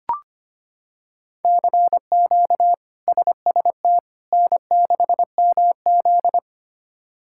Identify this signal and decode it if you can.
Morse code, heard as a keyed tone, about 25 wpm, 700 Hz. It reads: CQ SST N6MZ
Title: K1USN SST QSO Practice - 25wpm